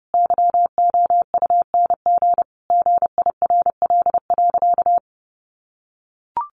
Morse code, heard as a keyed tone, about 30 wpm, 700 Hz. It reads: YOUNG GIRL.
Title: Sentences from Top 1400 Words 30wpm